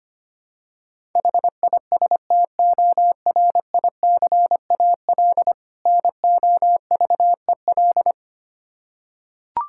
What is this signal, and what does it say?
Morse code, heard as a keyed tone, about 25 wpm, 700 Hz. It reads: HISTORICAL NOVEL
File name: practice / 25wpm / Difficult Words with Hints - 25wpm